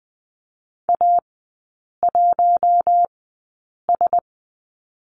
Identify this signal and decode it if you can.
Morse code, heard as a keyed tone, about 20 wpm, spaced as for 8 wpm, 700 Hz. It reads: A1S
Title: 041 Lesson 8e - Number 1 - Three Characters 20wpm